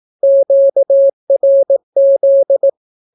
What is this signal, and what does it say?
Morse code, heard as a keyed tone, about 18 wpm, 550 Hz. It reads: QRZ
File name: naukapodstaw / skroty / skroty